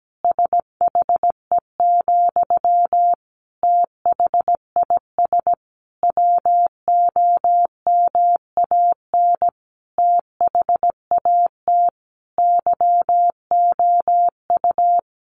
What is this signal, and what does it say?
Morse code, heard as a keyed tone, about 17 wpm, 700 Hz. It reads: SHE, THIS WOMAN THAT YOU
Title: Sentences from Top 1100 Words 17wpm